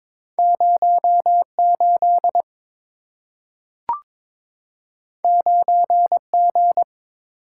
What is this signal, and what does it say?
Morse code, heard as a keyed tone, about 22 wpm, 690 Hz. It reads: 08 9G
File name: practice / 22wpm / Two Letter-Number Combinations - 22wpm